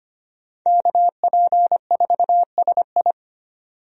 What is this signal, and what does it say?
Morse code, heard as a keyed tone, about 25 wpm, 700 Hz. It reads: KP4SI
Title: DXCC-100 Call Signs - 5 Characters - 25wpm